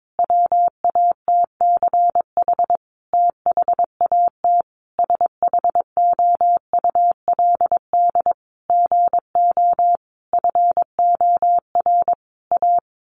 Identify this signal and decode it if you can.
Morse code, heard as a keyed tone, about 22 wpm, 700 Hz. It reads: WATCH THAT SHOULD GO FOR A
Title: Sentences from Top 1000 Words 22wpm